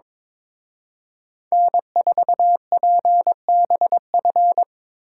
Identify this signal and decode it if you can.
Morse code, heard as a keyed tone, about 25 wpm, 700 Hz. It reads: N4PBF
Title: Speed Racing - USA 1x3 Call Signs 22wpm